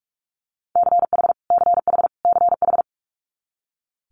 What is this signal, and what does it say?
Morse code, heard as a keyed tone, about 46 wpm, 700 Hz. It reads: CH CH CH
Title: Multi-letter Phonemes Repeated 3x 45wpm